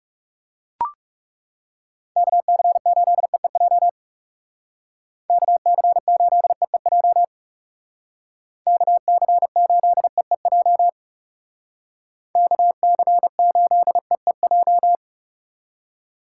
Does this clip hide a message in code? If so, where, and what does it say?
Yes: Morse code from 2.2 to 14.9 s, heard as a keyed tone, about 39 wpm, 700 Hz. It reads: KC8EEJ KC8EEJ KC8EEJ KC8EEJ